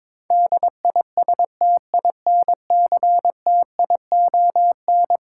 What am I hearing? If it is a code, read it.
Morse code, heard as a keyed tone, about 22 wpm, 700 Hz. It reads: DISTINCTION
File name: practice / 22wpm / Eleven Letter Words - 22wpm